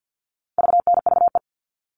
Morse code, heard as a keyed tone, about 48 wpm, 700 Hz. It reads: VNVE
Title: All Cut Numbers - 4-Digit 50wpm